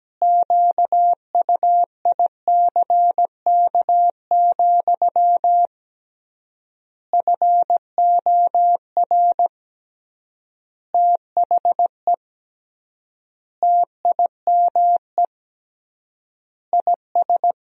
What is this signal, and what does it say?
Morse code, heard as a keyed tone, about 17 wpm, 700 Hz. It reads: QUICK, FOR THE TIME IS
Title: 3x Word-Spacing - Sentences from Top 500 Words - 17wpm